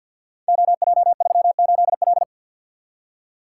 Morse code, heard as a keyed tone, about 50 wpm, 700 Hz. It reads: OJ38L
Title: Code Groups - 5 Letter:Numbers - 50wpm